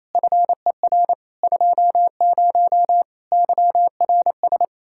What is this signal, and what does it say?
Morse code, heard as a keyed tone, about 27 wpm, 700 Hz. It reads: FER 20 YRS
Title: K1USN SST QSO Practice 28wpm